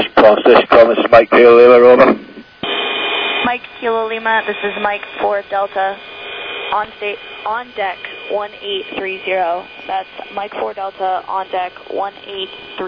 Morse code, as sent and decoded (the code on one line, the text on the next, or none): none